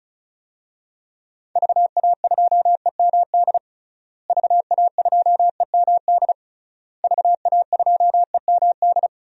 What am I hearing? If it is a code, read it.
Morse code, heard as a keyed tone, about 36 wpm, 700 Hz. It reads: VA2EMD VA2EMD VA2EMD
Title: DXCC-100 Call Signs - 6 Characters - Repeated 3x - 35wpm